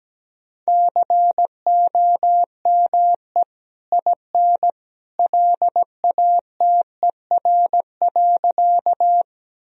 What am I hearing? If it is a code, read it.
Morse code, heard as a keyed tone, about 17 wpm, 700 Hz. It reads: COME IN LATER.
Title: Sentences from Top 1000 Words - Review Part 4 - 17wpm